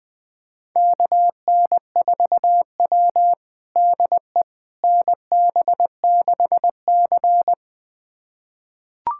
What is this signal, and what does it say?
Morse code, heard as a keyed tone, about 20 wpm, 700 Hz. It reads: KN4W DE NB6C
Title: USA Callsign DE Callsign - 4 Characters - 20wpm